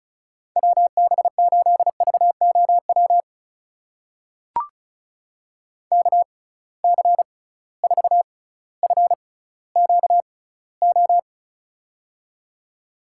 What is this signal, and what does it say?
Morse code, heard as a keyed tone, about 36 wpm, spaced as for 18 wpm, 700 Hz. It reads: WB8VOW KC4FQO
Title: USA Call Signs - 6 Characters - 6x Character Spacing - 35wpm